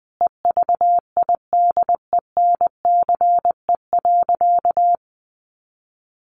Morse code, heard as a keyed tone, about 20 wpm, 700 Hz. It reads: EVIDENCE.
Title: Sentences from Top 1300 Words 20wpm